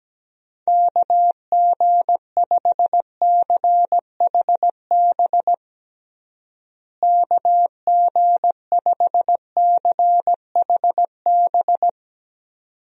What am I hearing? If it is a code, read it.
Morse code, heard as a keyed tone, about 17 wpm, 700 Hz. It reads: KG5CHB KG5CHB
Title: USA Call Signs - 6 Characters - Repeated 2x - 17wpm